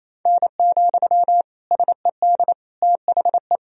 Morse code, heard as a keyed tone, about 28 wpm, 700 Hz. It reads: N, SED THE
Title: Sentences from Top 300 Words and Abbreviations 28wpm